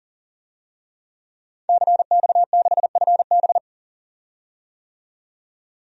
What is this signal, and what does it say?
Morse code, heard as a keyed tone, about 40 wpm, 700 Hz. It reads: CX6FB
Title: DXCC-100 Call Signs - 5 Characters - 40wpm